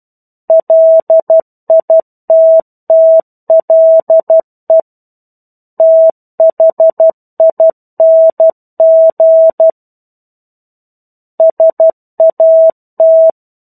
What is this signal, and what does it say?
Morse code, heard as a keyed tone, about 12 wpm, 650 Hz. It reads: LITTLE THING SAT
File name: 01-Chapter-01 12 WPM